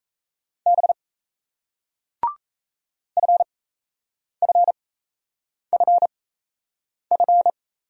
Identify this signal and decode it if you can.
Morse code, heard as a keyed tone, about 38 wpm, spaced as for 8 wpm, 700 Hz. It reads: B FFFF